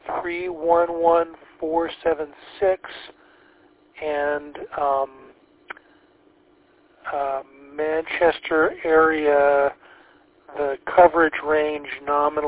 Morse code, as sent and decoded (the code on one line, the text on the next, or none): none